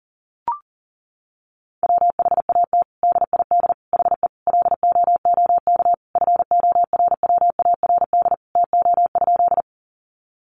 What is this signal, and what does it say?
Morse code, heard as a keyed tone, about 40 wpm, 700 Hz. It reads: WHAT DID HE LOOK FORWARD TO?
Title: Sentences from Top 1300 Words 40wpm